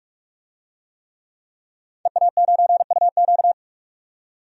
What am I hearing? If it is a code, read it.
Morse code, heard as a keyed tone, about 41 wpm, 700 Hz. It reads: EA9UQ
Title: DXCC-100 Call Signs - 5 Characters - 45wpm